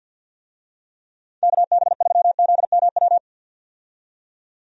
Morse code, heard as a keyed tone, about 48 wpm, 700 Hz. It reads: KB3ZMW
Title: USA Call Signs - 6 Characters - 50wpm